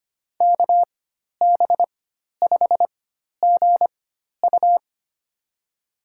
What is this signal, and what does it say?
Morse code, heard as a keyed tone, about 25 wpm, spaced as for 12 wpm, 700 Hz. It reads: KB5GU
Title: USA Call Signs - 5 Characters - 4x Character Spacing - 25wpm